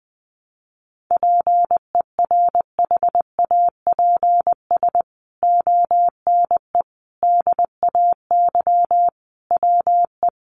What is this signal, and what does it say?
Morse code, heard as a keyed tone, about 20 wpm, 700 Hz. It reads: PERHAPS ONE DAY WE
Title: Sentences from Top 600 Words 20wpm